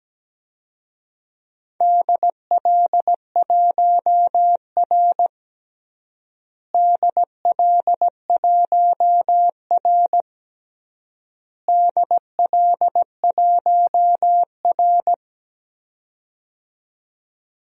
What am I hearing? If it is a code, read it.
Morse code, heard as a keyed tone, about 17 wpm, 700 Hz. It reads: DL1R DL1R DL1R
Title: DXCC-100 Call Signs - 4 Characters - Repeated 3x - 17wpm